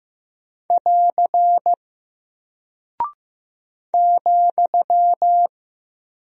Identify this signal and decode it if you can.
Morse code, heard as a keyed tone, about 15 wpm, 700 Hz. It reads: <AR> ,
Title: Single Character (Letter:Number:Punctuation:Prosign) Alt 15wpm